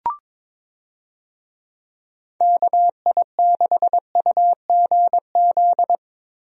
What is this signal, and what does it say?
Morse code, heard as a keyed tone, about 22 wpm, 700 Hz. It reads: KI6UGZ
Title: USA Call Signs - 6 Characters - Repeated 2x - 22wpm